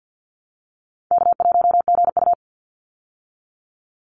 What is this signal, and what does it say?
Morse code, heard as a keyed tone, about 49 wpm, 700 Hz. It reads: K1GU
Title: Call Signs Repeated 2x - 50wpm